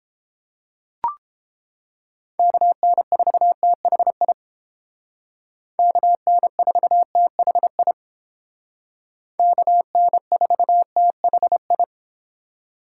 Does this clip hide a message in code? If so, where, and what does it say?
Yes: Morse code from 2.4 to 11.8 s, heard as a keyed tone, about 29 wpm, 700 Hz. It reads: KN4THI KN4THI KN4THI